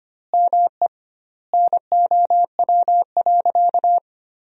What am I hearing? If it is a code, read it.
Morse code, heard as a keyed tone, about 25 wpm, 700 Hz. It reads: ME NOW.
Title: 2x Word-Spacing - Sentences from Top 500 Words - 25wpm